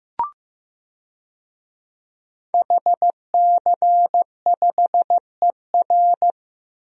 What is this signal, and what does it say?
Morse code, heard as a keyed tone, about 15 wpm, 700 Hz. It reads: HC5ER
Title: DXCC-100 Call Signs - 5 Characters - Repeated 2x - 15wpm